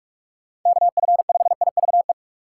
Morse code, heard as a keyed tone, about 46 wpm, 700 Hz. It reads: KF5IVE
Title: USA Call Signs - 2x Character Spacing - 45wpm